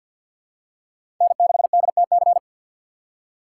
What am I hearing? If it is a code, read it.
Morse code, heard as a keyed tone, about 47 wpm, 700 Hz. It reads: N6DTC
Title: USA Call Signs - 5 Characters - 4x Character Spacing - 50wpm